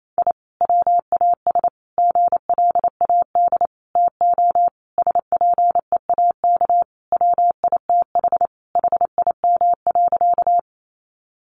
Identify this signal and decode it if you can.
Morse code, heard as a keyed tone, about 28 wpm, 700 Hz. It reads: I WAS GLAD TO SPEAK WITH HIM.
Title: Sentences from Top 1100 Words 28wpm